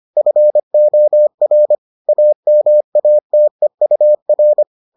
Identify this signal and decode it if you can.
Morse code, heard as a keyed tone, about 25 wpm, 600 Hz. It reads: FOR AMATEUR